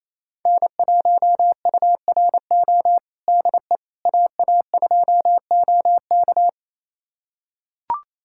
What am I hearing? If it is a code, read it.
Morse code, heard as a keyed tone, about 28 wpm, 700 Hz. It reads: N1URO DE AA2OK
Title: USA Callsign DE Callsign - 5 Characters - 28wpm